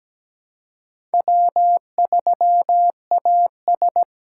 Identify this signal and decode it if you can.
Morse code, heard as a keyed tone, about 17 wpm, 700 Hz. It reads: W3AS
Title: USA Call Signs - 2x Character Spacing - 17wpm